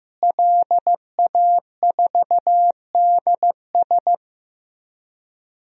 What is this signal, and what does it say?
Morse code, heard as a keyed tone, about 16 wpm, 700 Hz. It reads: LA4DS
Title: DXCC-100 Call Signs - 5 Characters - Repeated 3x - 15wpm